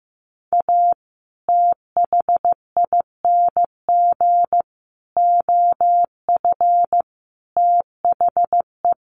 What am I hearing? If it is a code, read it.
Morse code, heard as a keyed tone, about 15 wpm, 700 Hz. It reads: A THING OF THE
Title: Sets of 5 Words Encore 15wpm